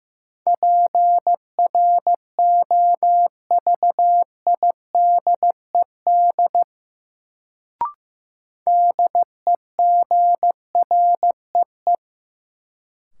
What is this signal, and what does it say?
Morse code, heard as a keyed tone, about 15 wpm, 700 Hz. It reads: PROVIDED DEGREE
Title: Sentences from Top 1500 Words - New Word Drill - 15wpm